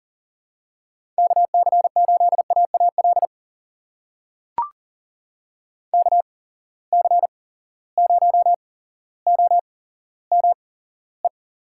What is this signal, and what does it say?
Morse code, heard as a keyed tone, about 39 wpm, spaced as for 17 wpm, 700 Hz. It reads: KC8AAL KC0OME
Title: USA Call Signs - 6 Characters - 8x Character Spacing - 40wpm